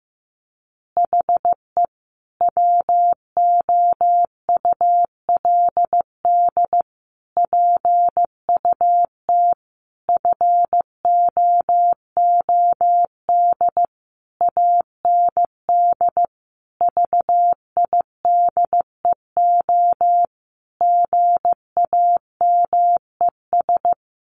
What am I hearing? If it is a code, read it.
Morse code, heard as a keyed tone, about 15 wpm, 700 Hz. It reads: HE WOULD PUT FOOD AND VIDEO GAMES
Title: Easy Phrases 15wpm